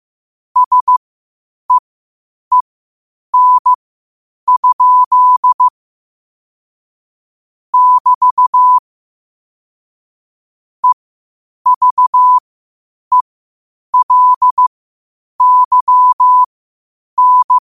Morse code, heard as a keyed tone, about 15 wpm, spaced as for 8 wpm, 1000 Hz. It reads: SEEN? <BT> EVELYN
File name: qotdfiles / 8wpm